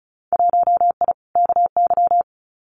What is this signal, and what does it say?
Morse code, heard as a keyed tone, about 36 wpm, 690 Hz. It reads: 1I KY
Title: Winter Field Day Exchanges 35wpm